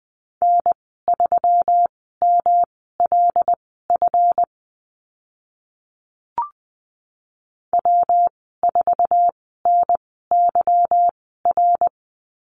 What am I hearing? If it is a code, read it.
Morse code, heard as a keyed tone, about 20 wpm, spaced as for 14 wpm, 700 Hz. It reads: N3MLF W4NYR
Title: 161 Lesson 26g - Letter g - Call Signs 20wpm